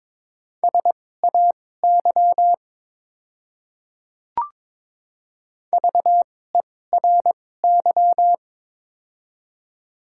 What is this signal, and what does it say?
Morse code, heard as a keyed tone, about 22 wpm, spaced as for 16 wpm, 700 Hz. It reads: SAY VERY
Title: 2x Character Spacing - Top 300 Words - 22wpm